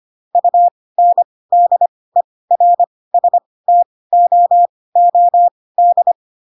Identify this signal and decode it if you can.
Morse code, heard as a keyed tone, about 25 wpm, spaced as for 18 wpm, 700 Hz. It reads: UNDERSTOOD
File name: Pinocchio / Pinocchio ch7 12 25 ik1hge